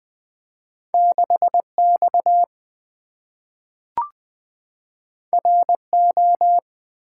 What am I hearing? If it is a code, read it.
Morse code, heard as a keyed tone, about 20 wpm, 700 Hz. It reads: 6X RO
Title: Two Letter-Number Combinations - 20wpm